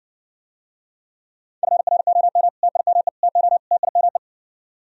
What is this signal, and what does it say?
Morse code, heard as a keyed tone, about 60 wpm, 700 Hz. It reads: FROM TIME TO TIME